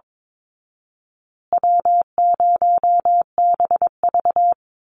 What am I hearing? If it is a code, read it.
Morse code, heard as a keyed tone, about 22 wpm, 700 Hz. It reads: W0BV
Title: Easy Call Signs Repeated 3x - 22wpm